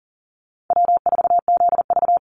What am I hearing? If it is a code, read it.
Morse code, heard as a keyed tone, about 42 wpm, 700 Hz. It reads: W4ZV
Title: NA SOTA Chaser Callsigns 40wpm